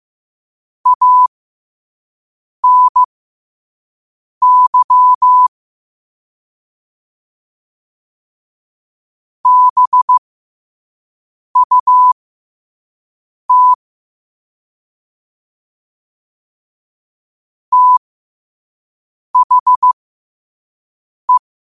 Morse code, heard as a keyed tone, about 15 wpm, spaced as for 5 wpm, 1000 Hz. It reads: ANY BUT THE
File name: qotdfiles / 5wpm